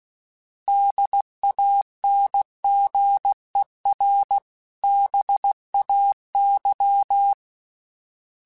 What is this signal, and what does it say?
Morse code, heard as a keyed tone, about 16 wpm, 800 Hz. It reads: DANGER BAY
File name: fromSMs / FromSMs mp3s / Cities@15